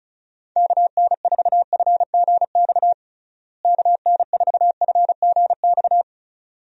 Morse code, heard as a keyed tone, about 36 wpm, 700 Hz. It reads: KN4FGX KN4FGX
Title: USA Call Signs - 6 Characters - Repeated 2x - 35wpm